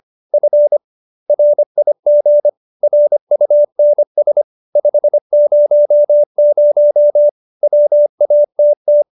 Morse code, heard as a keyed tone, about 25 wpm, 590 Hz. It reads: F RIG RUNS 500 WATT